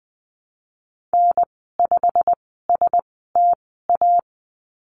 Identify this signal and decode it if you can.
Morse code, heard as a keyed tone, about 20 wpm, spaced as for 15 wpm, 700 Hz. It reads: N5STA
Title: 088 Lesson 15g - Number 5 - Call Signs 20wpm